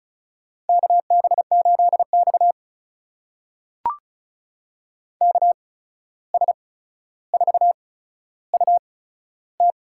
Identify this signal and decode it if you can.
Morse code, heard as a keyed tone, about 36 wpm, spaced as for 14 wpm, 700 Hz. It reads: KB8X KS4UT
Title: USA Call Signs - 8x Character Spacing - 35wpm